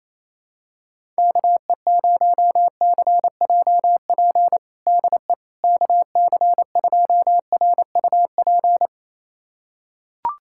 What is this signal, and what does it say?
Morse code, heard as a keyed tone, about 28 wpm, 700 Hz. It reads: KE0CJP DE KC2RUP
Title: USA Callsign DE Callsign - 6 Characters - 28wpm